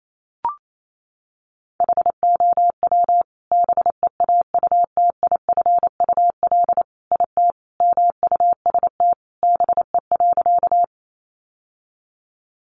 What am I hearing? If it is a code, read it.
Morse code, heard as a keyed tone, about 28 wpm, 700 Hz. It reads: HOW BEAUTIFUL IT MUST BE.
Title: Sentences from Top 900 Words 28wpm